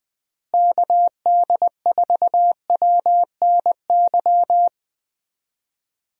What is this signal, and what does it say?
Morse code, heard as a keyed tone, about 20 wpm, 700 Hz. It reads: KD4WNY
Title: USA Call Signs - 6 Characters - Repeated 2x - 20wpm